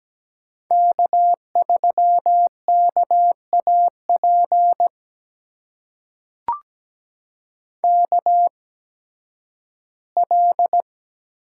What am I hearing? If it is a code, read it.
Morse code, heard as a keyed tone, about 17 wpm, spaced as for 11 wpm, 700 Hz. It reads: K3KAP KL
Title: USA Call Signs - 5 Characters - 8x Character Spacing - 17wpm